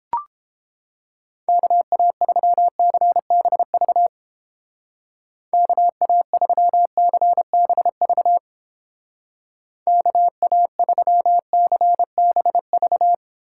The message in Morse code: KA3CBV KA3CBV KA3CBV

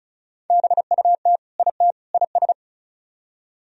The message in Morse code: BUT IT IS

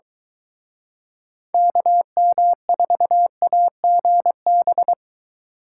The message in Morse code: KM4AGB

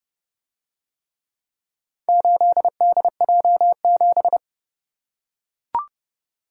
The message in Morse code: 8DJ7

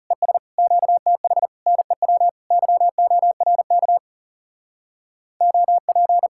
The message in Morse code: ES QTH NEW YORK OP

